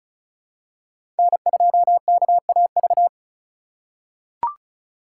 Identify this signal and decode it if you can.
Morse code, heard as a keyed tone, about 36 wpm, 700 Hz. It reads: N2KAV
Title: USA Call Signs - 5 Characters - 2x Character Spacing - 35wpm